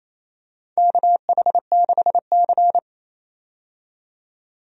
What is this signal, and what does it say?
Morse code, heard as a keyed tone, about 28 wpm, 700 Hz. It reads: KH6C
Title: DXCC-100 Call Signs - 4 Characters - 28wpm